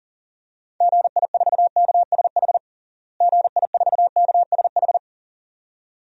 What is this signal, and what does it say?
Morse code, heard as a keyed tone, about 37 wpm, 700 Hz. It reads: GI4KSH GI4KSH
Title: DXCC-100 Call Signs - 6 Characters - Repeated 2x - 40wpm